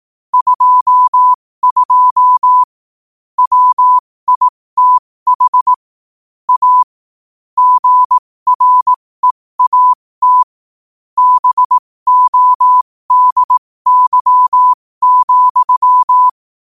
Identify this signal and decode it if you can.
Morse code, heard as a keyed tone, about 18 wpm, spaced as for 15 wpm, 1000 Hz. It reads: 22 WITH A GREAT BODY,
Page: New quotes every day in morse code at 15 Words per minute.